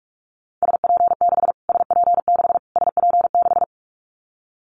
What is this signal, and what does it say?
Morse code, heard as a keyed tone, about 44 wpm, 700 Hz. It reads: SP6 SP6 SP6
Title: Callsign Prefixes for Top 25 DXCC - Repeated 3x - 45wpm